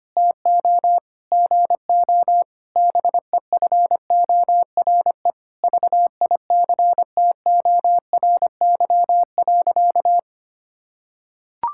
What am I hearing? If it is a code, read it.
Morse code, heard as a keyed tone, about 25 wpm, 700 Hz. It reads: TO GO BEFORE VICTORY.
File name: practice / 25wpm / Sentences with 9-Word Phrases - 25wpm